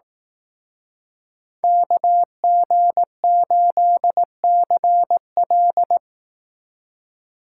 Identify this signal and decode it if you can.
Morse code, heard as a keyed tone, about 18 wpm, 700 Hz. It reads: KG8CL
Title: Speed Racing - USA 2x2 Call Signs 15wpm